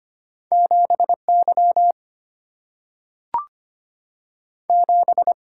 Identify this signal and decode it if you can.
Morse code, heard as a keyed tone, about 25 wpm, 700 Hz. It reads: 7 Y 7
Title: Single-Letter Single-Number 25wpm